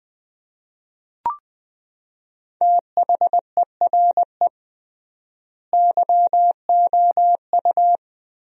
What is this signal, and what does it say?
Morse code, heard as a keyed tone, about 20 wpm, 700 Hz. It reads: THERE YOU
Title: 3x Word-Spacing - Sentences from Top 100 Words - 20wpm